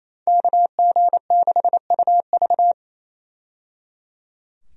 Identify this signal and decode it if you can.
Morse code, heard as a keyed tone, about 28 wpm, 700 Hz. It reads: KG6UV
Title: USA Call Signs - 5 Characters - 28wpm